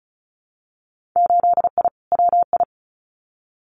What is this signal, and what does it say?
Morse code, heard as a keyed tone, about 36 wpm, 690 Hz. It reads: 8I WI
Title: Winter Field Day Exchanges 35wpm